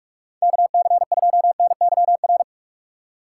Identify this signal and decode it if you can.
Morse code, heard as a keyed tone, about 46 wpm, 700 Hz. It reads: KC2NYR
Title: USA Call Signs - 6 Characters - 6x Character Spacing - 45wpm